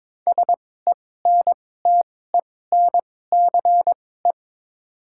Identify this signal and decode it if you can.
Morse code, heard as a keyed tone, about 22 wpm, spaced as for 16 wpm, 700 Hz. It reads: SENTENCE
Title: 2x Character Spacing - Top 500 Words - 22wpm